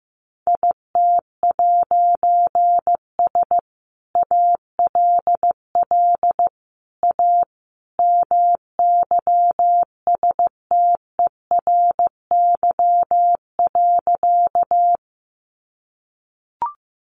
Morse code, heard as a keyed tone, about 15 wpm, 700 Hz. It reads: IT'S ALL A MYSTERY.
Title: A Sound Of Thunder 15wpm